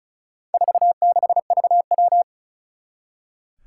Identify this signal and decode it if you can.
Morse code, heard as a keyed tone, about 36 wpm, 700 Hz. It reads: 46VW
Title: Four Challenging Letter-Number Combinations 35wpm